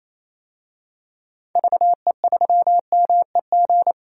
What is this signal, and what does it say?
Morse code, heard as a keyed tone, about 28 wpm, 700 Hz. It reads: VE3MEG
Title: DXCC-100 Call Signs - 6 Characters - 28wpm